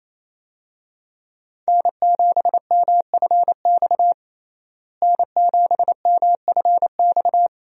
Morse code, heard as a keyed tone, about 28 wpm, 700 Hz. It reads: N7MFX N7MFX
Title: USA Call Signs - 5 Characters - Repeated 2x - 28wpm